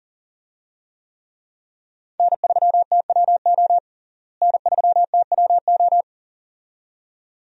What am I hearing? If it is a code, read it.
Morse code, heard as a keyed tone, about 40 wpm, 700 Hz. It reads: N3TWO N3TWO
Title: USA Call Signs - 5 Characters - Repeated 2x - 40wpm